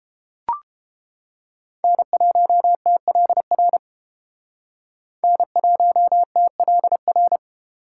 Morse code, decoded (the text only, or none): N1TLR N1TLR